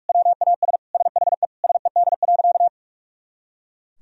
Morse code, heard as a keyed tone, about 41 wpm, 700 Hz. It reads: WAS SHE SED.
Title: Sentences from Top 400 Words and Abbreviations 45wpm